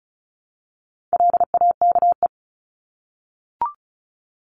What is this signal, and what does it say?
Morse code, heard as a keyed tone, about 34 wpm, 700 Hz. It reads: LAKE
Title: Top 1001-1250 Words 35wpm